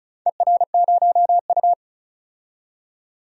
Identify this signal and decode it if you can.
Morse code, heard as a keyed tone, about 34 wpm, 700 Hz. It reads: ER0U
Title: DXCC-100 Call Signs - 4 Characters - 35wpm